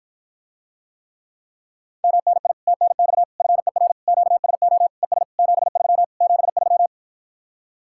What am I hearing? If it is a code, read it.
Morse code, heard as a keyed tone, about 54 wpm, 700 Hz. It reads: MNI TNX FER QSO ES 73 73